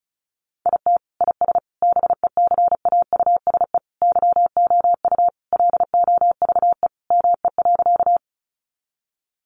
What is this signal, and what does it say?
Morse code, heard as a keyed tone, about 34 wpm, 700 Hz. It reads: IT IS BECAUSE YOU LOVE ME.
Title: Sentences from Top 600 Words 35wpm